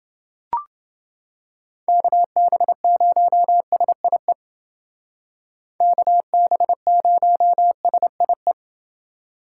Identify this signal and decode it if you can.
Morse code, heard as a keyed tone, about 29 wpm, 700 Hz. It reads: KB0SIE KB0SIE